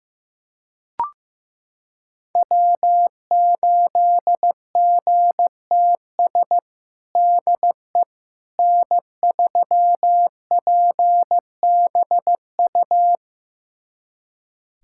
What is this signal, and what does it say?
Morse code, heard as a keyed tone, about 15 wpm, 700 Hz. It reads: W8GTS DE N3PBU
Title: USA Callsign DE Callsign - 5 Characters - 15wpm